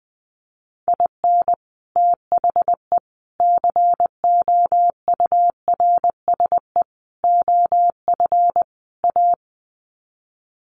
Morse code, heard as a keyed tone, about 20 wpm, 700 Hz. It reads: IN THE COURSE OF A
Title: Sets of 5 Words Encore2 20wpm